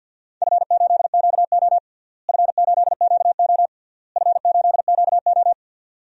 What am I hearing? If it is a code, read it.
Morse code, heard as a keyed tone, about 53 wpm, 700 Hz. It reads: F8QO F8QO F8QO
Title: DXCC-100 Call Signs - 4 Characters - Repeated 3x - 50wpm